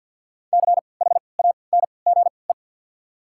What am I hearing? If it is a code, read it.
Morse code, heard as a keyed tone, about 53 wpm, spaced as for 29 wpm, 690 Hz. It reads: CHANGE
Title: 3x Character Spacing - Top 300 Words - 50wpm